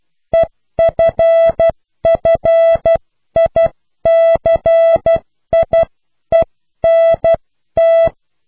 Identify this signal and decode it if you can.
Morse code, heard as a keyed tone, about 12 wpm, 660 Hz. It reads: EFFICIENT